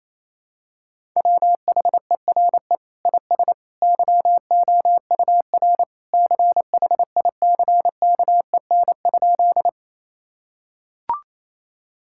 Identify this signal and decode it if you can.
Morse code, heard as a keyed tone, about 28 wpm, 700 Hz. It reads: WHERE IS YOUR CHICKEN?
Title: Sentences from Top 1000 Words - Review Part 5 - 28wpm